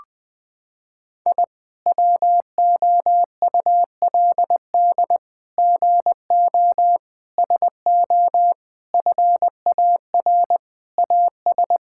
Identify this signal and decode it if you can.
Morse code, heard as a keyed tone, about 20 wpm, 700 Hz. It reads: I WOULD GO SO FAR AS
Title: Sets of 8 Words - 20wpm